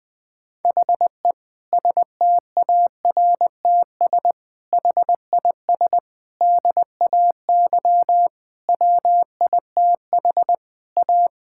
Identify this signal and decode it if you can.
Morse code, heard as a keyed tone, about 20 wpm, 700 Hz. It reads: HE STARTS HIS DAY WITH A
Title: Difficult Sentences with Hints - 20wpm